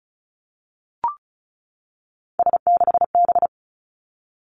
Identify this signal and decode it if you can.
Morse code, heard as a keyed tone, about 36 wpm, 700 Hz. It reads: S6B
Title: Three Challenging Letter-Number Combinations 35wpm